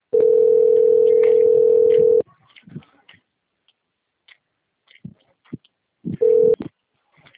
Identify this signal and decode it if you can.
Ringback tone (North America), 440 + 480 Hz.
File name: out- - - - - .69170